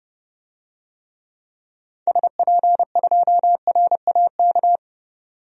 Morse code, heard as a keyed tone, about 31 wpm, 700 Hz. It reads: SP2RAK
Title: DXCC-100 Call Signs - 6 Characters - 30wpm